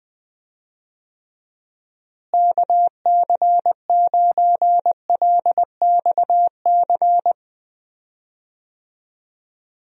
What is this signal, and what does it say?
Morse code, heard as a keyed tone, about 20 wpm, 700 Hz. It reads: KC9LXC
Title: USA Call Signs - 6 Characters - Repeated 2x - 20wpm